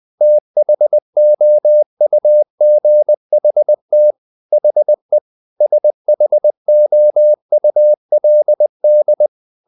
Morse code, heard as a keyed tone, about 20 wpm, 600 Hz. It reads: THOUGHT HE SHOULD
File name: the old man and the sea part 2 20wpm